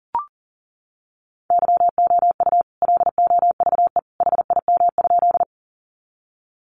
Morse code, heard as a keyed tone, about 40 wpm, 700 Hz. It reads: YOU LOVE HIM?
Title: Sentences from Top 400 Words 40wpm